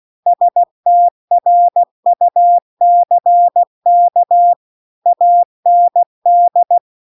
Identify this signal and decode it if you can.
Morse code, heard as a keyed tone, about 16 wpm, 700 Hz. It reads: STRUCK AND
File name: CW-The-Big-Twoheaded-River-2-16WPM